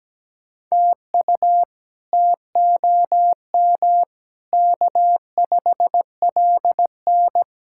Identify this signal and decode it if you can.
Morse code, heard as a keyed tone, about 17 wpm, 700 Hz. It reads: TU TOM K5LN
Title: K1USN SST QSO Practice - 17wpm